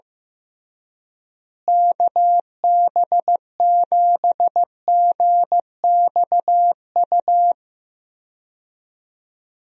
Morse code, heard as a keyed tone, about 15 wpm, 700 Hz. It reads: KB7GXU
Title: USA Call Signs - 6 Characters - Repeated 3x - 15wpm